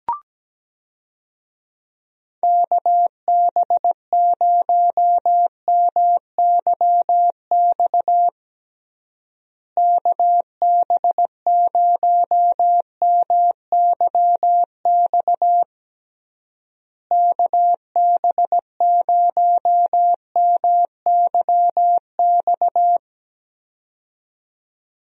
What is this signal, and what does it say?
Morse code, heard as a keyed tone, about 17 wpm, 700 Hz. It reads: KB0MYX KB0MYX KB0MYX
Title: USA Call Signs - 6 Characters - Repeated 3x - 17wpm